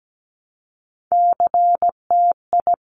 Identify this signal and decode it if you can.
Morse code, heard as a keyed tone, about 17 wpm, 700 Hz. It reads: CTI
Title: Top Three Letter Combinations in Words 17wpm